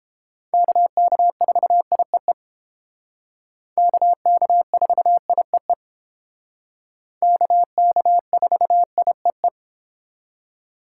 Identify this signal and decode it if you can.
Morse code, heard as a keyed tone, about 32 wpm, 700 Hz. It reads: KK4IEE KK4IEE KK4IEE